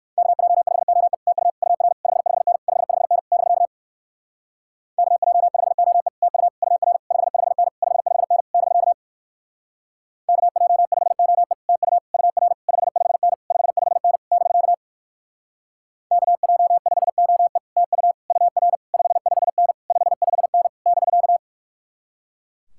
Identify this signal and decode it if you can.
Morse code, heard as a keyed tone, about 59 wpm, 700 Hz. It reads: KJ5OE TU UR 55N 55N <BK> KJ5OE TU UR 55N 55N <BK> KJ5OE TU UR 55N 55N <BK> KJ5OE TU UR 55N 55N <BK>